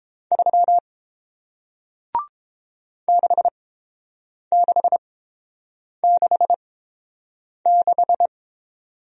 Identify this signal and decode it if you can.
Morse code, heard as a keyed tone, about 30 wpm, spaced as for 7 wpm, 690 Hz. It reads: 3 6666